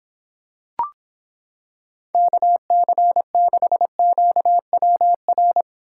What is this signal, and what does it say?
Morse code, heard as a keyed tone, about 26 wpm, 700 Hz. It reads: KC6QWR